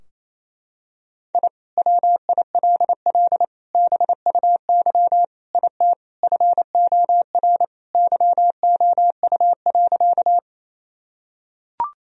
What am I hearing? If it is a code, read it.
Morse code, heard as a keyed tone, about 28 wpm, 700 Hz. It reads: I WILL BUY IT FOR YOU.
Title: Sentences from Top 1000 Words - Review Part 4 - 28wpm